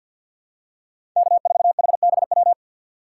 Morse code, heard as a keyed tone, about 53 wpm, 700 Hz. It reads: K4HBW
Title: USA Call Signs - 5 Characters - 6x Character Spacing - 50wpm